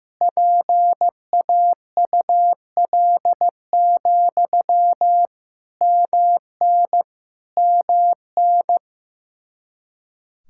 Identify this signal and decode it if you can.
Morse code, heard as a keyed tone, about 15 wpm, 700 Hz. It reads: PAUL, MN MN
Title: Practice QSO - Exchange 1 - Beginning - 15wpm